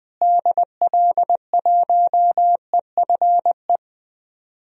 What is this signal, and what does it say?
Morse code, heard as a keyed tone, about 20 wpm, 700 Hz. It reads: DL1EFE
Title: DXCC-100 Call Signs - 6 Characters - Repeated 2x - 20wpm